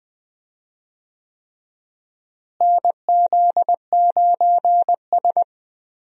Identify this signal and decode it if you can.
Morse code, heard as a keyed tone, about 20 wpm, 700 Hz. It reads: NZ9S
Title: USA Call Signs - 4 Characters - Repeated 3x - 20wpm